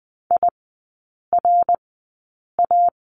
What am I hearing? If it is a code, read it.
Morse code, heard as a keyed tone, about 20 wpm, spaced as for 8 wpm, 700 Hz. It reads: IRA
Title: 074 Lesson 13e - Letter L - Three Characters 20wpm